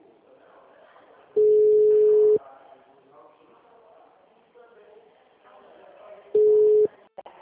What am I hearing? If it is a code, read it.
Ringback tone (Europe, 425 Hz), with speech.
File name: out- -2015- - - .9275